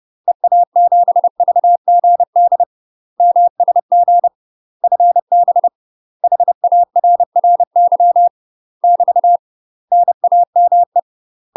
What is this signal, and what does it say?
Morse code, heard as a keyed tone, about 30 wpm, 700 Hz. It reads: EA7VGD MSG FB HARRY <BT> NAME
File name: UK MASTER QSOs TRAINING / 30WPM / UK MASTER QSOs 0013